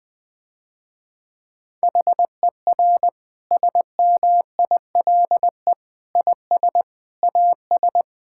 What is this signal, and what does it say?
Morse code, heard as a keyed tone, about 21 wpm, 700 Hz. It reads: HER SMILE IS AS
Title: Difficult Sentences with Hints - 20wpm